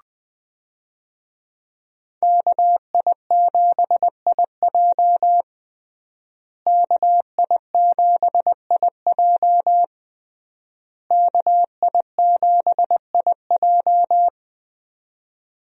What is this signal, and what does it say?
Morse code, heard as a keyed tone, about 20 wpm, 700 Hz. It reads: KI7IJ KI7IJ KI7IJ
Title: USA Call Signs - 5 Characters - Repeated 3x - 20wpm